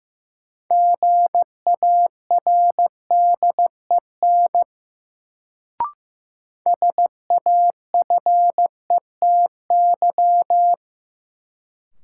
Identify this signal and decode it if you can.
Morse code, heard as a keyed tone, about 15 wpm, 700 Hz. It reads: GARDEN SAFETY
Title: Sentences from Top 700 Words - New Word Drill - 15wpm